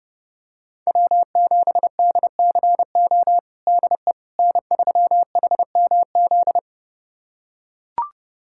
Morse code, heard as a keyed tone, about 32 wpm, 700 Hz. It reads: W7DCO DE N3HMZ
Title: USA Callsign DE Callsign - 5 Characters - 30wpm